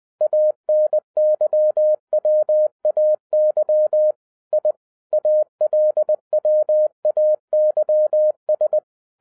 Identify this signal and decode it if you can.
Morse code, heard as a keyed tone, about 20 wpm, 610 Hz. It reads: ANYWAY I ALWAYS
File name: 2012-01-mp3 / n16-THROUGH EIGHT MONTHS-20wpm